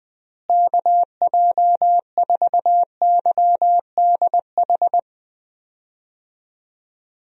Morse code, heard as a keyed tone, about 20 wpm, 700 Hz. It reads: KJ4YDH
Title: USA Call Signs - 6 Characters - Repeated 2x - 20wpm